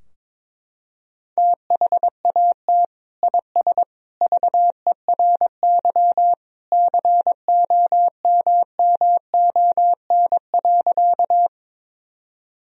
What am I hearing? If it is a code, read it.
Morse code, heard as a keyed tone, about 22 wpm, 700 Hz. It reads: THAT IS VERY COMMON.
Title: Sentences from Top 1000 Words - Review Part 3 - 22wpm